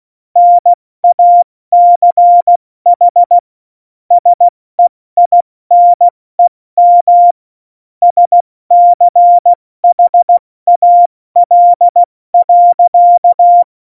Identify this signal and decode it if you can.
Morse code, heard as a keyed tone, about 16 wpm, 700 Hz. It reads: NACH SEINEM SCHAL.
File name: Morsekurs / Witze / Witz 22. Dezember 2020, 70 F80